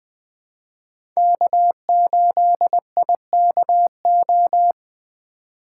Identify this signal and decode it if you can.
Morse code, heard as a keyed tone, about 20 wpm, 700 Hz. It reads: K8IKO
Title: 2023 POTA Hunter Call Signs - 20wpm